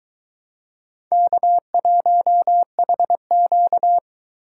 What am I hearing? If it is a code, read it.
Morse code, heard as a keyed tone, about 23 wpm, 700 Hz. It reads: K1HQ